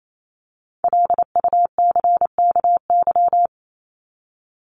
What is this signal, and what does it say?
Morse code, heard as a keyed tone, about 28 wpm, 700 Hz. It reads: LUCKY
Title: Top 1751-2000 Words 28wpm